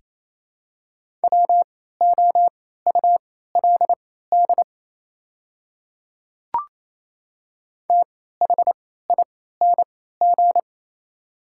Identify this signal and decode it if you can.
Morse code, heard as a keyed tone, about 28 wpm, spaced as for 16 wpm, 690 Hz. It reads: WOULD THING
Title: 3x Character Spacing - Top 100 Words - 28wpm